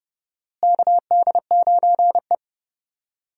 Morse code, heard as a keyed tone, about 30 wpm, 700 Hz. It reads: KD9E
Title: USA Call Signs - 4 Characters - Repeated 3x - 30wpm